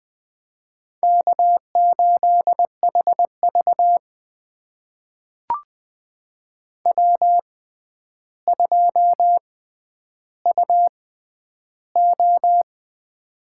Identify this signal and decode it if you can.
Morse code, heard as a keyed tone, about 20 wpm, spaced as for 10 wpm, 700 Hz. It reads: K8HV W2UO
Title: USA Call Signs - 4 Characters - 6x Character Spacing - 20wpm